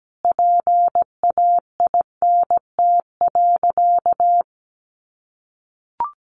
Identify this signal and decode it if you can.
Morse code, heard as a keyed tone, about 17 wpm, 700 Hz. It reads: PAINT.
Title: A Sound Of Thunder 17wpm